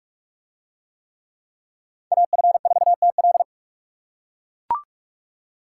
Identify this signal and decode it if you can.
Morse code, heard as a keyed tone, about 44 wpm, 700 Hz. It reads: AF4TL